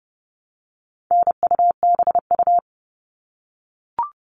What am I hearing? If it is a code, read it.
Morse code, heard as a keyed tone, about 30 wpm, 700 Hz. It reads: NUBU
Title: All Cut Numbers - 4-Digit 30wpm